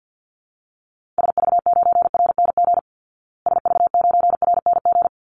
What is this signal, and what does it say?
Morse code, heard as a keyed tone, about 50 wpm, 700 Hz. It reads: SV9RNG SV9RNG
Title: Call Signs Repeated 3x - 50wpm